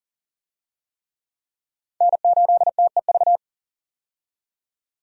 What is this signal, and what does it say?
Morse code, heard as a keyed tone, about 40 wpm, 700 Hz. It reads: N8TEV
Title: USA Call Signs - 5 Characters - 40wpm